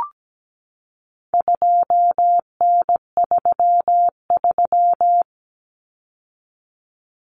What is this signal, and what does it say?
Morse code, heard as a keyed tone, about 17 wpm, 700 Hz. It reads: 2N33
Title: Common Cut Numbers - 4-Digit 17wpm